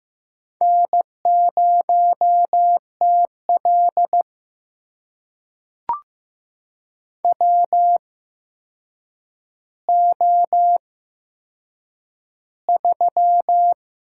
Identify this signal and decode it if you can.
Morse code, heard as a keyed tone, about 16 wpm, spaced as for 7 wpm, 700 Hz. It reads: N0TL WO3
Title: USA Call Signs - 4 Characters - 8x Character Spacing - 15wpm